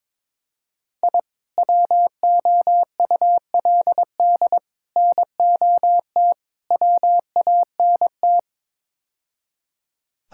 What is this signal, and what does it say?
Morse code, heard as a keyed tone, about 22 wpm, 700 Hz. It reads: I WOULD NOT WANT
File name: practice / 22wpm / Build A Phrase - 5-Word Phrases - 22wpm